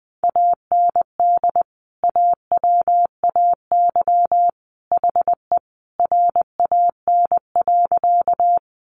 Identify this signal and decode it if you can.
Morse code, heard as a keyed tone, about 20 wpm, 700 Hz. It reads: AND AWAY HE RAN.
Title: Sentences from Top 600 Words 20wpm